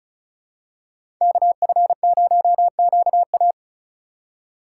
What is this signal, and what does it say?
Morse code, heard as a keyed tone, about 34 wpm, 700 Hz. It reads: KF0QA
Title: USA Call Signs - 5 Characters - 2x Character Spacing - 35wpm